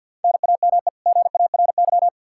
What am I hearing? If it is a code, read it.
Morse code, heard as a keyed tone, about 51 wpm, 700 Hz. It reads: NAME GARY
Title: Practice QSO - Exchange 1 - Beginning - 50wpm